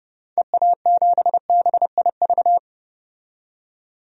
Morse code, heard as a keyed tone, about 31 wpm, 690 Hz. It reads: EA7BIV
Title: DXCC-100 Call Signs - 6 Characters - 30wpm